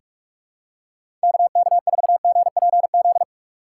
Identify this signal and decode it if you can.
Morse code, heard as a keyed tone, about 44 wpm, 700 Hz. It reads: KK4GPZ
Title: USA Call Signs - 6 Characters - 3x Character Spacing - 45wpm